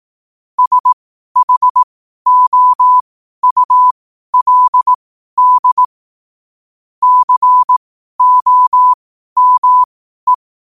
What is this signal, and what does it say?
Morse code, heard as a keyed tone, about 18 wpm, spaced as for 12 wpm, 1000 Hz. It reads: SHOULD COME
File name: qotdfiles / 12wpm